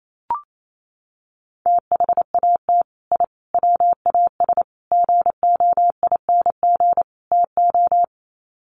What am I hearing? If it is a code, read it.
Morse code, heard as a keyed tone, about 28 wpm, 700 Hz. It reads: THAT I WAS GOING TO
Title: Sets of 5 Words Encore 28wpm